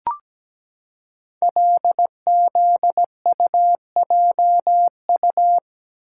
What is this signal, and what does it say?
Morse code, heard as a keyed tone, about 17 wpm, 700 Hz. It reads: LZUJU
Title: Code Groups - 5 Letters - 17wpm